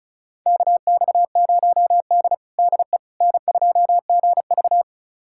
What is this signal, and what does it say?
Morse code, heard as a keyed tone, about 36 wpm, 700 Hz. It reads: KX0D DE N2GV
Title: USA Callsign DE Callsign - 4 Characters - 35wpm